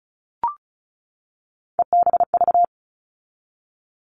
Morse code, heard as a keyed tone, about 34 wpm, 700 Hz. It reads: EBV
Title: All Cut Numbers - 3-Digit 35wpm